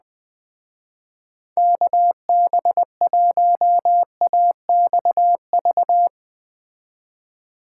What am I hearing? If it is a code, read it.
Morse code, heard as a keyed tone, about 20 wpm, 700 Hz. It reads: KB1AXV
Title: Speed Racing - USA 2x3 Call Signs 17wpm